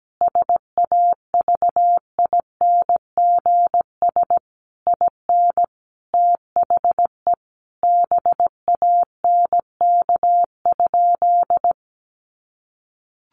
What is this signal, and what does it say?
Morse code, heard as a keyed tone, about 17 wpm, 700 Hz. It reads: SAVINGS IN THE BANK?
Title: Sentences from Top 1200 Words 17wpm